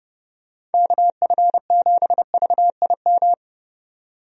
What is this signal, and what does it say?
Morse code, heard as a keyed tone, about 32 wpm, 700 Hz. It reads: KF7VIM
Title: USA Call Signs - 6 Characters - 30wpm